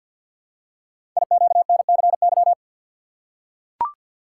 Morse code, heard as a keyed tone, about 53 wpm, 700 Hz. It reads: IQNCY
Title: Code Groups - 5 Letters - 50wpm